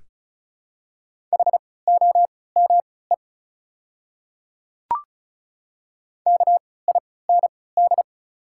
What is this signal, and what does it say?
Morse code, heard as a keyed tone, about 34 wpm, spaced as for 20 wpm, 700 Hz. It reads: HOME KIND
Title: 3x Character Spacing - Top 200 Words - 35wpm